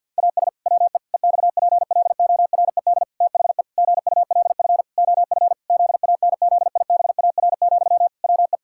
Morse code, heard as a keyed tone, about 50 wpm, 700 Hz. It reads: AS WE EXPLORED THE GULF OF ZANZIBAR, WE
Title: Pangram Challenge v2 - 50wpm